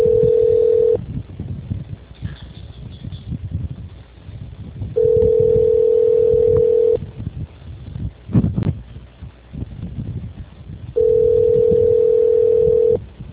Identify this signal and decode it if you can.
Ringback tone (North America), 440 + 480 Hz.
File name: out- -2036- - - .53